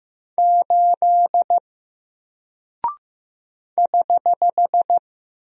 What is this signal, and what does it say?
Morse code, heard as a keyed tone, about 15 wpm, 700 Hz. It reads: 8 <HH>
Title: Single Character (Letter:Number:Punctuation:Prosign) 15wpm